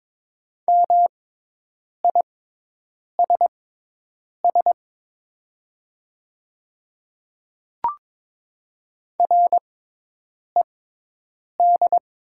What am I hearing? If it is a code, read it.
Morse code, heard as a keyed tone, about 22 wpm, spaced as for 8 wpm, 700 Hz. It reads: MISS RED
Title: 6x Character Spacing - Top 500 Words - 22wpm